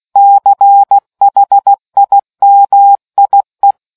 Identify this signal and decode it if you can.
Morse code, heard as a keyed tone, about 16 wpm, 800 Hz. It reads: CHIMIE